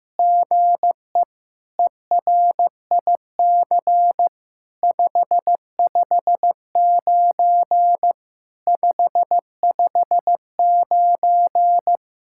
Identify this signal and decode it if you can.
Morse code, heard as a keyed tone, about 15 wpm, 700 Hz. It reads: GE ERIC 559 559
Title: SOTA QSO Practice 15wpm